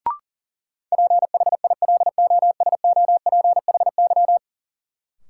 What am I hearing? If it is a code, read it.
Morse code, heard as a keyed tone, about 37 wpm, 700 Hz. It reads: PHILOSOPHY
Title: Sentences from Top 900 Words - New Word Drill - 40wpm